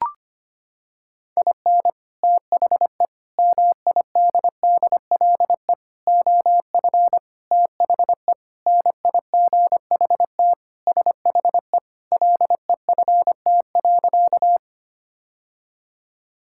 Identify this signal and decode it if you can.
Morse code, heard as a keyed tone, about 25 wpm, 700 Hz. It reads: IN THE MIDDLE OF THE NIGHT SHE LEFT.
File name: practice / 25wpm / Sentences with 3-Word Phrases - 25wpm